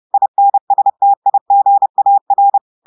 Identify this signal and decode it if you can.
Morse code, heard as a keyed tone, about 30 wpm, 800 Hz. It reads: INSTIGAR
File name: audios / portugais / portugais texte en clair 2 vitesse 30 mots minutes